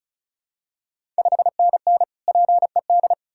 Morse code, heard as a keyed tone, about 36 wpm, 700 Hz. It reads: 5NN PED
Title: Ontario QSO Party Exchange 35wpm